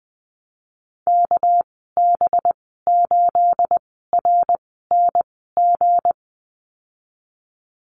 Morse code, heard as a keyed tone, about 20 wpm, spaced as for 15 wpm, 700 Hz. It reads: KB8RNG
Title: 238 Lesson 38g - Letter q - Call Signs 20wpm